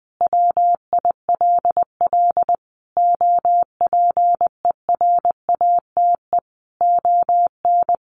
Morse code, heard as a keyed tone, about 19 wpm, 700 Hz. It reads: WILL OPERATE ON
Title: Sentences from Top 1500 Words 20wpm